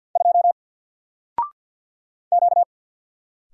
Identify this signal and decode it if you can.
Morse code, heard as a keyed tone, about 50 wpm, 700 Hz. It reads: 2 Q
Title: Single Letter-Number 50wpm